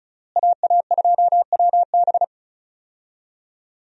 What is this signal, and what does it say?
Morse code, heard as a keyed tone, about 36 wpm, 700 Hz. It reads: AA2WB
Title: USA Call Signs - 5 Characters - Repeated 3x - 35wpm